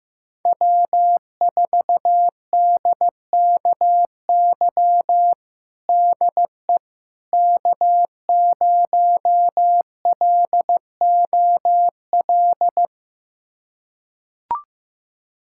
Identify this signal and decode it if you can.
Morse code, heard as a keyed tone, about 15 wpm, 700 Hz. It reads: W4DKY DE K0LOL
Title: USA Callsign DE Callsign - 5 Characters - 15wpm